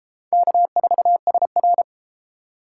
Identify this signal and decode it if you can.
Morse code, heard as a keyed tone, about 32 wpm, 700 Hz. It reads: K4SR